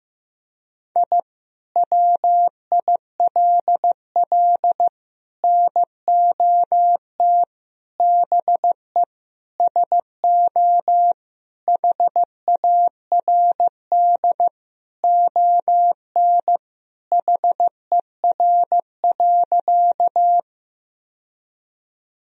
Morse code, heard as a keyed tone, about 15 wpm, 700 Hz. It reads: I WILL NOT BE SO HARD ON HER.
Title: Sentences from Top 1000 Words - Review Part 1 - 15wpm